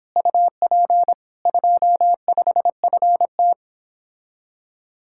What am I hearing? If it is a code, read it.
Morse code, heard as a keyed tone, about 29 wpm, 700 Hz. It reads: UP 25FT